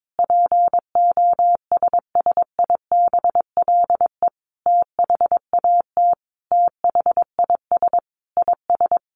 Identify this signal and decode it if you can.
Morse code, heard as a keyed tone, about 22 wpm, 700 Hz. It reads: POSSIBLE THAT THIS IS
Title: Sentences from Top 1200 Words 22wpm